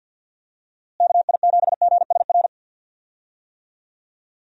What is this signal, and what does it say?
Morse code, heard as a keyed tone, about 51 wpm, 700 Hz. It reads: KI7GSR
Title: USA Call Signs - 6 Characters - Repeated 3x - 50wpm